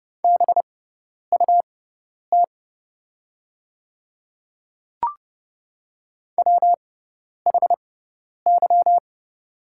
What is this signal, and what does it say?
Morse code, heard as a keyed tone, about 30 wpm, spaced as for 10 wpm, 690 Hz. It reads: BUT WHY
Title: 6x Character Spacing - Top 200 Words - 30wpm